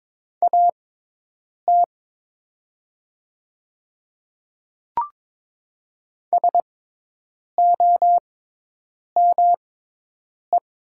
Morse code, heard as a keyed tone, about 22 wpm, spaced as for 8 wpm, 700 Hz. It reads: AT SOME
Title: 6x Character Spacing - Top 100 Words - 22wpm